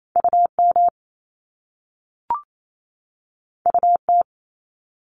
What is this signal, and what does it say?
Morse code, heard as a keyed tone, about 28 wpm, 700 Hz. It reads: UM UT
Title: Top Two Letter Combinations in Words 28wpm